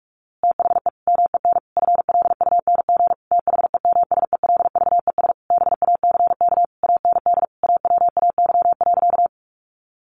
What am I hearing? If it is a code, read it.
Morse code, heard as a keyed tone, about 44 wpm, 700 Hz. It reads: THE MEN FLUNG THEMSELVES BACK AND AWAY.
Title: A Sound Of Thunder 45wpm